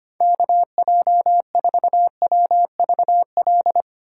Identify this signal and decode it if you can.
Morse code, heard as a keyed tone, about 25 wpm, 700 Hz. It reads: KJ4WVL
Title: USA Call Signs - 6 Characters - 2x Character Spacing - 25wpm